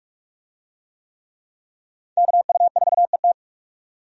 Morse code, heard as a keyed tone, about 44 wpm, 700 Hz. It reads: KU4ET
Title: USA Call Signs - 5 Characters - 45wpm